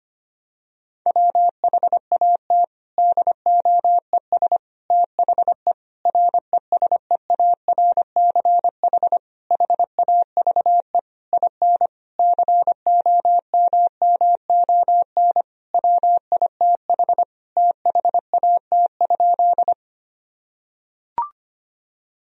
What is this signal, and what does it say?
Morse code, heard as a keyed tone, about 25 wpm, 700 Hz. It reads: WHAT DOES THE RESEARCH HAVE IN COMMON WITH THAT?
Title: Sentences from Top 1000 Words - Review Part 4 - 25wpm